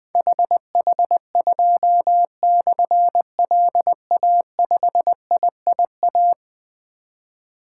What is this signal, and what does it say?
Morse code, heard as a keyed tone, about 21 wpm, 700 Hz. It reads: HH2/LA5IIA
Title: Difficult Call Signs 20wpm